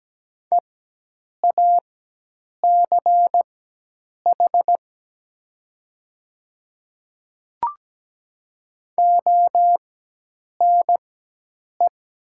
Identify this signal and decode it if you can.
Morse code, heard as a keyed tone, about 17 wpm, spaced as for 8 wpm, 700 Hz. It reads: EACH ONE
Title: 4x Character Spacing - Top 300 Words - 17wpm